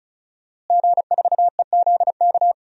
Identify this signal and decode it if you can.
Morse code, heard as a keyed tone, about 36 wpm, 700 Hz. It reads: G4EZK
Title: DXCC-100 Call Signs - 5 Characters - Repeated 2x - 35wpm